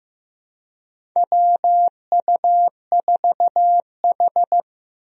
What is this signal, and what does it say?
Morse code, heard as a keyed tone, about 15 wpm, 700 Hz. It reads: WU4H
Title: USA Call Signs - 4 Characters - 15wpm